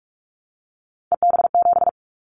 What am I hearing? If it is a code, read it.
Morse code, heard as a keyed tone, about 46 wpm, 700 Hz. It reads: EB7
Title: Callsign Prefixes for Top 25 DXCC - 45wpm